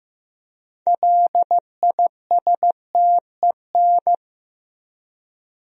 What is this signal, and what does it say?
Morse code, heard as a keyed tone, about 14 wpm, 700 Hz. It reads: LISTEN
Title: 4x Word-Spacing - Sentences from Top 400 Words - 15wpm